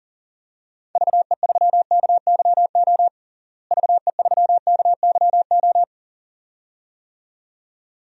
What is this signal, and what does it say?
Morse code, heard as a keyed tone, about 40 wpm, 700 Hz. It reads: VE3KYO VE3KYO
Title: DXCC-100 Call Signs - 6 Characters - Repeated 2x - 40wpm